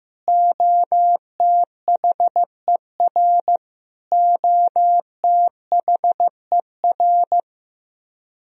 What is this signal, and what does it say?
Morse code, heard as a keyed tone, about 15 wpm, 700 Hz. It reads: OTHER OTHER
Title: Top 100 Words Repeated 15wpm